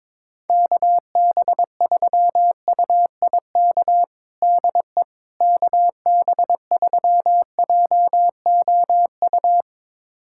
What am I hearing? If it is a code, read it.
Morse code, heard as a keyed tone, about 22 wpm, 700 Hz. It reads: KB3UIK DE KB3JOU
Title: USA Callsign DE Callsign - 6 Characters - 22wpm